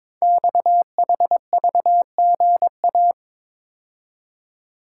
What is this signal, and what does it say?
Morse code, heard as a keyed tone, about 22 wpm, 700 Hz. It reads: XHVGA
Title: Code Groups - 5 Letters - 22wpm